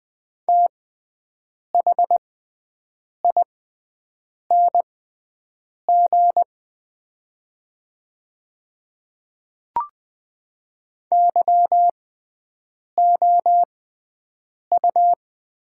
Morse code, heard as a keyed tone, about 20 wpm, spaced as for 7 wpm, 700 Hz. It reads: THING YOU
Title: 6x Character Spacing - Top 100 Words - 20wpm